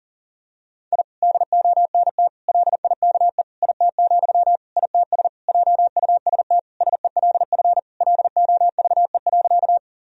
Morse code, heard as a keyed tone, about 40 wpm, 700 Hz. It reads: I DONT LIKE IT, ITS JUST SELF LOVE.
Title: Sentences from Top 1000 Words - Review Part 1 - 40wpm